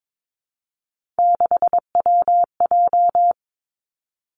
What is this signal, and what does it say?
Morse code, heard as a keyed tone, about 22 wpm, 700 Hz. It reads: 6WJ
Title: Three Challenging Letter-Number Combinations 22wpm